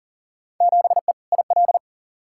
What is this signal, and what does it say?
Morse code, heard as a keyed tone, about 40 wpm, 700 Hz. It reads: 7E IL
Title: Field Day Exchanges Encore - 40wpm